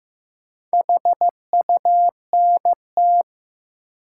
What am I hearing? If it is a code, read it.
Morse code, heard as a keyed tone, about 15 wpm, 700 Hz. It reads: HUNT
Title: Four Letter Words - 15wpm